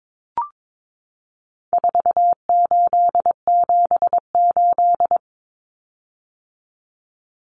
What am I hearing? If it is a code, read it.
Morse code, heard as a keyed tone, about 22 wpm, 700 Hz. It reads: 4878
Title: Common Cut Numbers - 4-Digit 22wpm